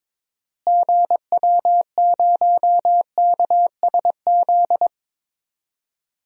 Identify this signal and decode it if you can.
Morse code, heard as a keyed tone, about 22 wpm, 700 Hz. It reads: GW0KSZ
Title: DXCC-100 Call Signs - 6 Characters - Repeated 3x - 22wpm